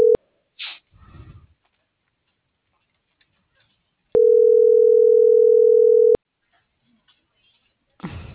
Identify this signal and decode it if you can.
Ringback tone (North America), 440 + 480 Hz.